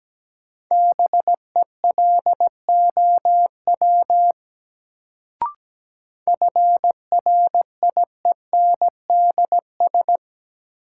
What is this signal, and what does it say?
Morse code, heard as a keyed tone, about 17 wpm, 700 Hz. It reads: BELOW FRIENDS
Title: Sentences from Top 1000 Words - Review Part 5 - New Word Drill - 17wpm